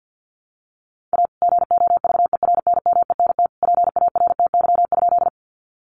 Morse code, heard as a keyed tone, about 50 wpm, 700 Hz. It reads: A GOVERNMENT PARTY?
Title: Sentences from Top 1400 Words 50wpm